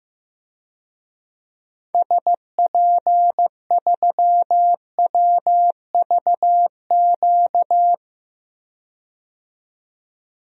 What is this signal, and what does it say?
Morse code, heard as a keyed tone, about 16 wpm, 700 Hz. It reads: SP3WVQ
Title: DXCC-100 Call Signs - 6 Characters - 15wpm